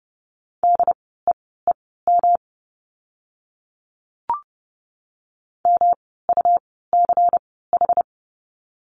Morse code, heard as a keyed tone, about 30 wpm, spaced as for 17 wpm, 700 Hz. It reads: DEEM MUCH
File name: course / 30:10wpm / 108 Lesson 18f - Letter m - Words 30wpm